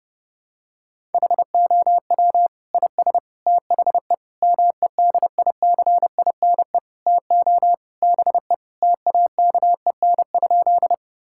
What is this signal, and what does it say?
Morse code, heard as a keyed tone, about 32 wpm, 700 Hz. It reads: HOW IS THE MEDICINE TO BE TAKEN?
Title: Sentences from Top 1000 Words - Review Part 6 - 30wpm